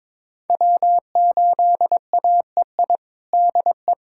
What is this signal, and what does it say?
Morse code, heard as a keyed tone, about 22 wpm, 700 Hz. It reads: W8AEI DE
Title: USA Callsign DE Callsign - 5 Characters - 22wpm